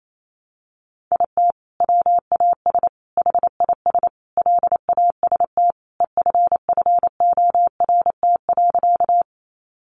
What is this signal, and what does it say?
Morse code, heard as a keyed tone, about 28 wpm, 700 Hz. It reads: IT WAS HIS LAST EFFORT.
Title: Sentences from Top 1200 Words 28wpm